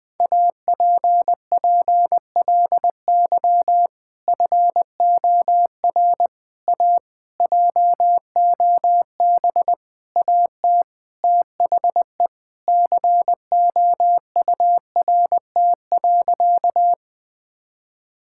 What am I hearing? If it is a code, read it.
Morse code, heard as a keyed tone, about 21 wpm, 700 Hz. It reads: APPLY FOR A JOB AT THE COURT.
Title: Sentences from Top 1600 Words 20wpm